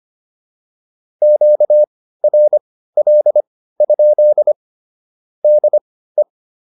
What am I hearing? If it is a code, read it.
Morse code, heard as a keyed tone, about 25 wpm, spaced as for 15 wpm, 600 Hz. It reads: QRL? DE